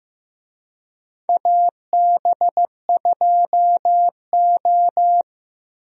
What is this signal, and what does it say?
Morse code, heard as a keyed tone, about 15 wpm, 700 Hz. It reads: AB2O
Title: USA Call Signs - 4 Characters - Repeated 3x - 15wpm